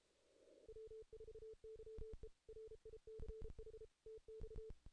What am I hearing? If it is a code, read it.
Morse code, heard as a keyed tone, about 32 wpm, 440 Hz. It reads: W4YE RICH TX
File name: 2016 JAN CW NAQP / 2016 Jan naqp k5ot - 0-8